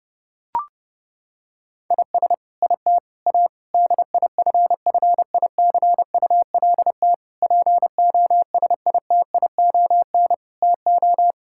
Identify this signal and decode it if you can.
Morse code, heard as a keyed tone, about 32 wpm, 700 Hz. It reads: IS IT A DIFFICULT POSITION TO
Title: Sentences from Top 1000 Words - Review Part 5 - 30wpm